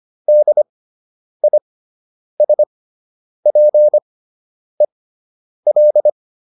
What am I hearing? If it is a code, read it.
Morse code, heard as a keyed tone, about 25 wpm, spaced as for 9 wpm, 600 Hz. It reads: DISPEL